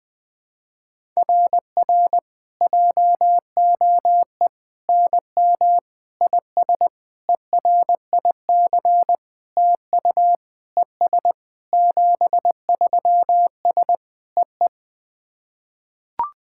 Morse code, heard as a keyed tone, about 21 wpm, 700 Hz. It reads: RR JOE NM IS ERIC TU ES 73S EE
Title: SOTA QSO Practice 20wpm